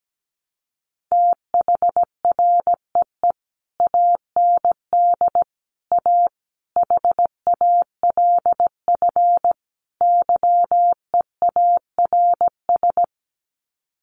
Morse code, heard as a keyed tone, about 17 wpm, 700 Hz. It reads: THREE AND A HALF YEARS
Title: Sets of 5 Words Encore2 17wpm